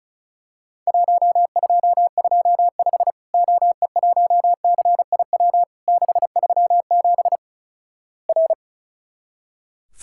Morse code, heard as a keyed tone, about 36 wpm, 700 Hz. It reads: 1225 OE1CIW 637 R
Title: Worked All Europe DX Contest - QTC Exchange - 35wpm